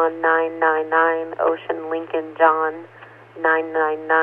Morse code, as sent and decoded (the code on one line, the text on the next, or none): none